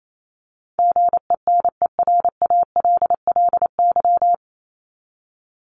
Morse code, heard as a keyed tone, about 28 wpm, 700 Hz. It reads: GENERALLY
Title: LY Suffix - Top 100 Words with Base Words 28wpm